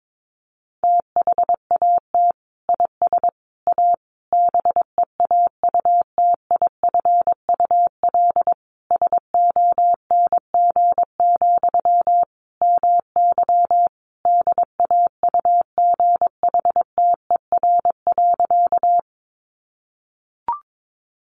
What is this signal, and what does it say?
Morse code, heard as a keyed tone, about 22 wpm, 700 Hz. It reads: THAT IS A BEAUTIFUL SONG, MY DAUGHTER.
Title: Sentences from Top 1500 Words 22wpm